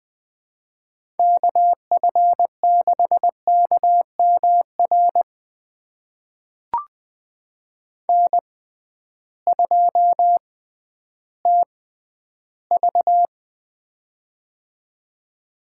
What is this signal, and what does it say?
Morse code, heard as a keyed tone, about 20 wpm, spaced as for 12 wpm, 700 Hz. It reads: KF6KMR N2TV
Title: USA Call Signs - 6x Character Spacing - 20wpm